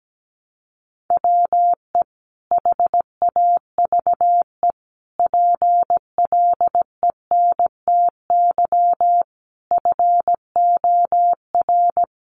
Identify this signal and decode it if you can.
Morse code, heard as a keyed tone, about 17 wpm, 700 Hz. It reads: WE HAVE PLENTY FOR
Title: Sentences from Top 1300 Words 17wpm